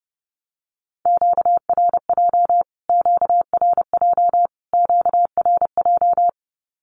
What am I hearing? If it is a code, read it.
Morse code, heard as a keyed tone, about 30 wpm, 700 Hz. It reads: QRJ QRJ QRJ
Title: Q-Codes Repeated 3x 30wpm